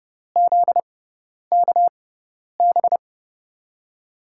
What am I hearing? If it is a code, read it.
Morse code, heard as a keyed tone, about 31 wpm, 700 Hz. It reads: Z K B
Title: USA Call Signs - 5 Characters - 6x Character Spacing - 30wpm